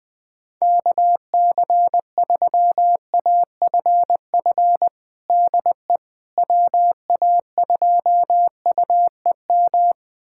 Morse code, heard as a keyed tone, about 20 wpm, 700 Hz. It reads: KC3AFF DE WA2UEM
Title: USA Callsign DE Callsign - 6 Characters - 20wpm